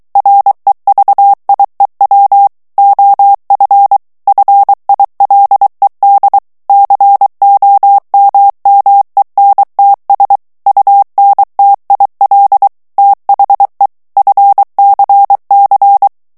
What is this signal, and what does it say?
Morse code, heard as a keyed tone, about 23 wpm, 800 Hz. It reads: REVIEW OF FILED COMMENTS UNTIL THE FCC
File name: ANGLAIS texte clair 3 vitesse 22 mots minute